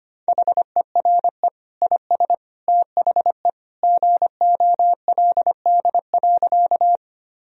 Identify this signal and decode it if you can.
Morse code, heard as a keyed tone, about 25 wpm, 700 Hz. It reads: HERE IS THE GOLD.
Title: Sentences from Top 1000 Words - Review Part 3 - 25wpm